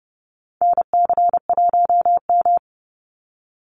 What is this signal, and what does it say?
Morse code, heard as a keyed tone, about 30 wpm, 700 Hz. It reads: NC1M
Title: Call Signs Repeated 2x - 30wpm